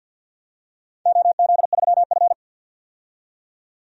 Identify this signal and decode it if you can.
Morse code, heard as a keyed tone, about 51 wpm, 700 Hz. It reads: OZ3F
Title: DXCC-100 Call Signs - 4 Characters - 50wpm